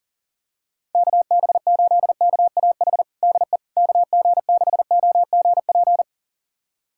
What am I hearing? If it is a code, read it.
Morse code, heard as a keyed tone, about 40 wpm, 700 Hz. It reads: KB8KAH DE KG6OGP
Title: USA Callsign DE Callsign - 6 Characters - 40wpm